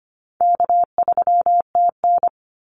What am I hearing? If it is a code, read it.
Morse code, heard as a keyed tone, about 25 wpm, 700 Hz. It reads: K3TN
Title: Easy Call Signs - 25wpm